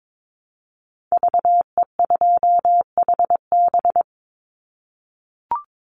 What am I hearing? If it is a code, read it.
Morse code, heard as a keyed tone, about 22 wpm, 700 Hz. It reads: VE2HB
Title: Call Signs - 22wpm